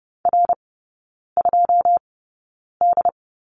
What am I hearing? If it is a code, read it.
Morse code, heard as a keyed tone, about 30 wpm, spaced as for 9 wpm, 690 Hz. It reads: R2D
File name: Lesson 16e - Letter c - Three Characters 30wpm